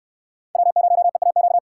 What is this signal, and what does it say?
Morse code, heard as a keyed tone, about 68 wpm, 700 Hz. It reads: W0EAQ